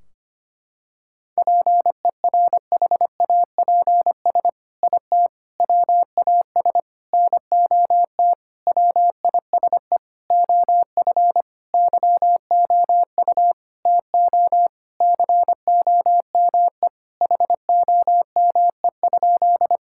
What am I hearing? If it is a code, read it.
Morse code, heard as a keyed tone, about 25 wpm, 700 Hz. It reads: PERHAPS IT WAS NOT WISE OF YOU TO COME HOME?
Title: Sentences from Top 1600 Words 25wpm